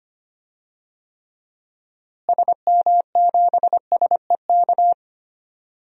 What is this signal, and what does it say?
Morse code, heard as a keyed tone, about 25 wpm, 700 Hz. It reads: SM7SEK
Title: DXCC-100 Call Signs - 6 Characters - Repeated 3x - 25wpm